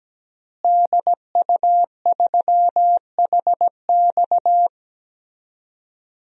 Morse code, heard as a keyed tone, about 17 wpm, 700 Hz. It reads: DU3HX
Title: DXCC-100 Call Signs - 5 Characters - Repeated 2x - 17wpm